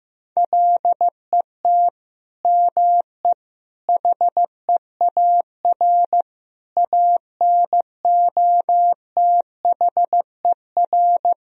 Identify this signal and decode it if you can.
Morse code, heard as a keyed tone, about 15 wpm, 700 Hz. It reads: LET ME HEAR ANOTHER
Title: Sentences from Top 1000 Words - Review Part 2 - 15wpm